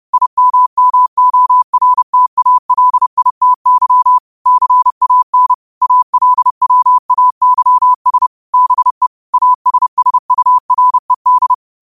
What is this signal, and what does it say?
Morse code, heard as a keyed tone, about 30 wpm, 1000 Hz. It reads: IMMORTALITY CAN ALWAYS BE ASSURED